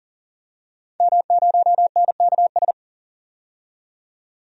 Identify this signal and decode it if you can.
Morse code, heard as a keyed tone, about 40 wpm, 700 Hz. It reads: M0NKS
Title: DXCC-100 Call Signs - 5 Characters - 40wpm